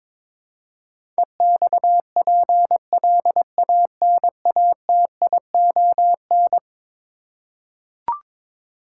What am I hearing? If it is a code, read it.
Morse code, heard as a keyed tone, about 22 wpm, 700 Hz. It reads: EXPLANATION
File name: practice / 22wpm / Eleven Letter Words - 22wpm